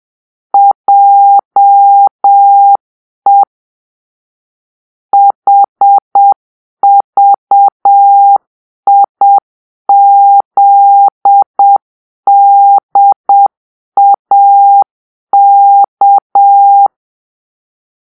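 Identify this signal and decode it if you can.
Morse code, heard as a keyed tone, about 7 wpm, 800 Hz. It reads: JE HVIZDAK
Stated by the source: Morse-code cipher made by a player team for the Kachna puzzle game.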